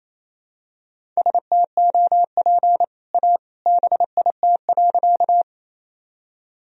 Morse code, heard as a keyed tone, about 28 wpm, 700 Hz. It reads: STOP A BIT.
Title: Sentences from Top 1000 Words - Review Part 1 - 28wpm